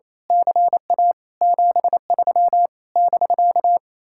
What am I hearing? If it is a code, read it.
Morse code, heard as a keyed tone, about 27 wpm, 700 Hz. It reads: CA 73 <BK>
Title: Speed-Racing Common QSO Phrases - 28wpm